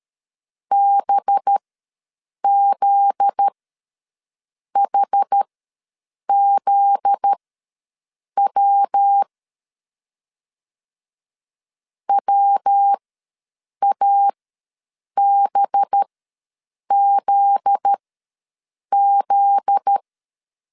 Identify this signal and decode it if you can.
Morse code, heard as a keyed tone, about 13 wpm, spaced as for 7 wpm, 800 Hz. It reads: BZHZW WABZZ
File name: LEZIONI / 7 GF9Z / 7 GF9Z ws 12wpm 3321